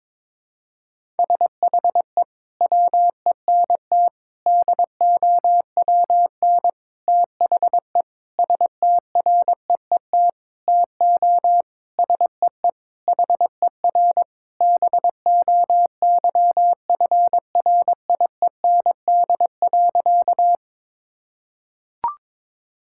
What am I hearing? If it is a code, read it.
Morse code, heard as a keyed tone, about 22 wpm, 700 Hz. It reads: SHE WENT DOWN THE STREET TO SEE HER BOYFRIEND.
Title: Sentences from Top 1000 Words - Review Part 5 - 22wpm